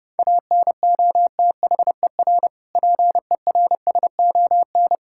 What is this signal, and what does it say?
Morse code, heard as a keyed tone, about 30 wpm, 700 Hz. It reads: ANOTHER PERSON
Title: Sentences from Top 1000 Words - Review Part 5 - 30wpm